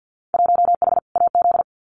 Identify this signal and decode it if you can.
Morse code, heard as a keyed tone, about 50 wpm, 700 Hz. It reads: 1H AZ
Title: Winter Field Day Exchanges 50wpm